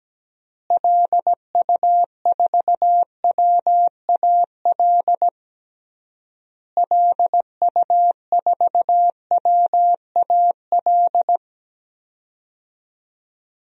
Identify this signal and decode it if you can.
Morse code, heard as a keyed tone, about 17 wpm, 700 Hz. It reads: LU4WAL LU4WAL
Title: DXCC-100 Call Signs - 6 Characters - Repeated 3x - 17wpm